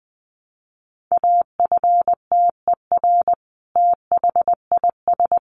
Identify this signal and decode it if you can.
Morse code, heard as a keyed tone, about 20 wpm, 700 Hz. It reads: AFTER THIS
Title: Sentences from Top 1200 Words 20wpm